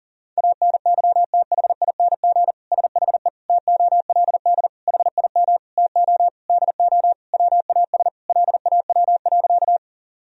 Morse code, heard as a keyed tone, about 40 wpm, 700 Hz. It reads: ANYTHING SHE TOLD HIM TO DO WAS LAW.
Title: Sentences from Top 1000 Words - Review Part 3 - 40wpm